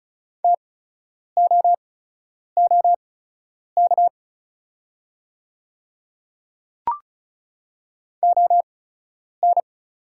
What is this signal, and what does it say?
Morse code, heard as a keyed tone, about 36 wpm, spaced as for 10 wpm, 700 Hz. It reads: TOOK ON
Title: 8x Character Spacing - Top 500 Words - 35wpm